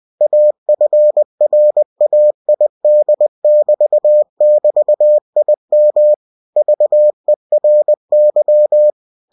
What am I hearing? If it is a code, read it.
Morse code, heard as a keyed tone, about 20 wpm, 600 Hz. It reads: AFRAID<BT><BT>IM VERY
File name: cw-books / winnie-the-pooh / winnie-20wpm-page-0055